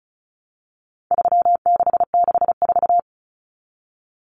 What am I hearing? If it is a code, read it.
Morse code, heard as a keyed tone, about 34 wpm, 700 Hz. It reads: 3664
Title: Common Cut Numbers - 4-Digit 35wpm